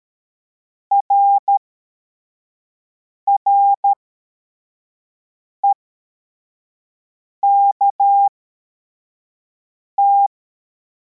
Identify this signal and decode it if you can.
Morse code, heard as a keyed tone, about 13 wpm, spaced as for 4 wpm, 800 Hz. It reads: RREKT
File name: formation / cw / Text23